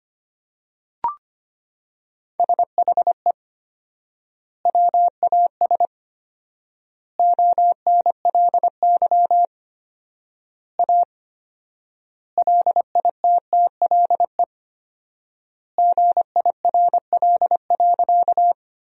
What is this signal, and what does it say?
Morse code, heard as a keyed tone, about 25 wpm, 700 Hz. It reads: SHE WAS ONLY A LITTLE GIRL.
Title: 4x Word-Spacing - Sentences from Top 400 Words - 25wpm